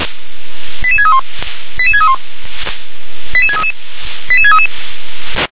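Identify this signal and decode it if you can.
Five-tone selective call (ZVEI), heard as quick runs of five tones, tones of about 70 ms. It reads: 88631 88631 88633 88633
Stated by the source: ZVEI five-tone call 88742 received on SDR.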